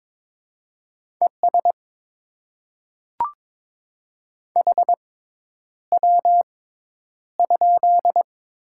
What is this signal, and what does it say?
Morse code, heard as a keyed tone, about 22 wpm, spaced as for 10 wpm, 690 Hz. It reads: ES HW?
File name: Top 20 QSO Elements - 6x Character Spacing - 22wpm